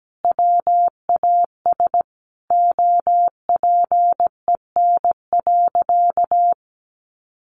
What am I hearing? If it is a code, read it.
Morse code, heard as a keyed tone, about 17 wpm, 700 Hz. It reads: WAS OPEN.
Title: Sentences from Top 1200 Words 17wpm